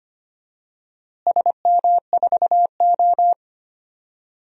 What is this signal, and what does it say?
Morse code, heard as a keyed tone, about 25 wpm, 700 Hz. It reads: SM4O
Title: DXCC-100 Call Signs - 4 Characters - Repeated 3x - 25wpm